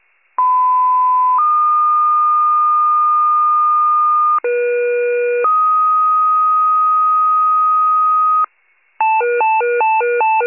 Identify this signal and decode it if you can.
Two-tone page, heard as a one-second tone, then a three-second tone, about 1010 then 1220 Hz, about 480 then 1250 Hz. It is heard from 0.4 to 8.5 s.
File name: pager / 98 RESCUE / wcalert